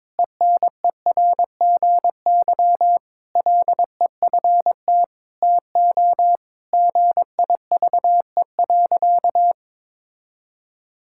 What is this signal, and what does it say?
Morse code, heard as a keyed tone, about 23 wpm, 700 Hz. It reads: ENERGY LEFT TO GIVE.
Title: Sentences from Top 1000 Words - Review Part 6 - 22wpm